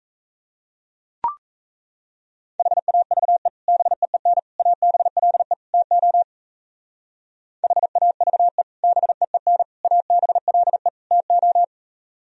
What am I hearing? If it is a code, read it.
Morse code, heard as a keyed tone, about 35 wpm, 700 Hz. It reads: HAVE BEEN ABLE TO HAVE BEEN ABLE TO
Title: Speed-Racing - Sets of 4 Words 28wpm